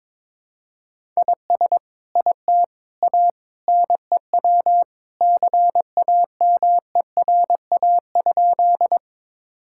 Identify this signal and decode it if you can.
Morse code, heard as a keyed tone, about 22 wpm, 700 Hz. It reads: IS IT A NEW CAMERA?
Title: Sentences from Top 1000 Words - Review Part 5 - 22wpm